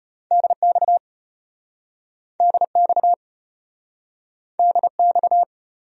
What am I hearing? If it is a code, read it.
Morse code, heard as a keyed tone, about 34 wpm, 700 Hz. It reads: DX DX DX